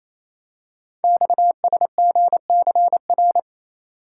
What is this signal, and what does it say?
Morse code, heard as a keyed tone, about 27 wpm, 700 Hz. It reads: XSGCR
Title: Code Groups - 5 Letters - 28wpm